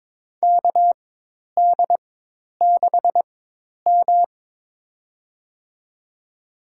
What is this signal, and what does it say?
Morse code, heard as a keyed tone, about 22 wpm, spaced as for 10 wpm, 700 Hz. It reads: KD6M
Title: USA Call Signs - 4 Characters - 4x Character Spacing - 22wpm